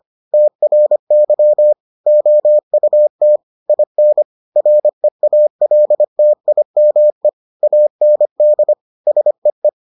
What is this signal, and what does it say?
Morse code, heard as a keyed tone, about 25 wpm, 600 Hz. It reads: TRY OUT IN REALTIME AND SEE